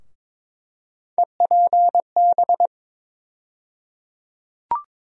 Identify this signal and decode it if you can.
Morse code, heard as a keyed tone, about 22 wpm, 700 Hz. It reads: EPB
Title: USA Call Sign Postfixes - 3 Letters 22wpm